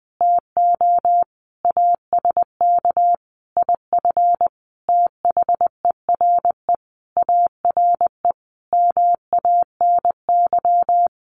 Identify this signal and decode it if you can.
Morse code, heard as a keyed tone, about 20 wpm, 700 Hz. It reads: TO ASK IF THERE ARE MANY
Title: Sentences from Top 1500 Words 20wpm